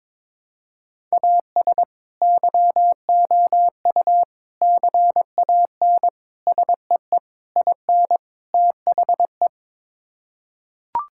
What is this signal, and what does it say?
Morse code, heard as a keyed tone, about 22 wpm, 700 Hz. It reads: AS YOU CAN SEE IN THE
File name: practice / 22wpm / Sets of 6 Words - 22wpm